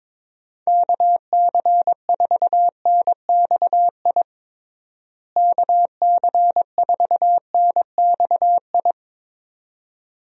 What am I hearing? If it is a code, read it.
Morse code, heard as a keyed tone, about 21 wpm, 700 Hz. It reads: KC4NXI KC4NXI
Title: USA Call Signs - 6 Characters - Repeated 2x - 22wpm